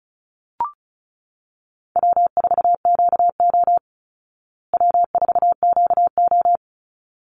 Morse code, heard as a keyed tone, about 34 wpm, 700 Hz. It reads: W4QO W4QO
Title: Call Signs Repeated 3x - 35wpm